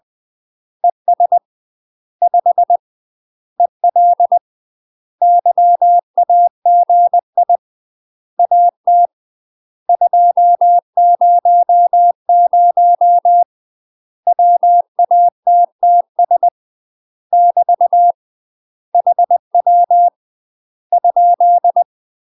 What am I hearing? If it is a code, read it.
Morse code, heard as a keyed tone, about 20 wpm, 700 Hz. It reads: ES 5 EL YAGI AT 200 WATTS <BT> HW ?
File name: UK MASTER QSOs TRAINING / 20WPM / UK MASTER QSOs 0070